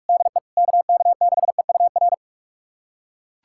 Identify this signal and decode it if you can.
Morse code, heard as a keyed tone, about 42 wpm, 700 Hz. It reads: DE KK6EUR
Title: USA Callsign DE Callsign - 6 Characters - 45wpm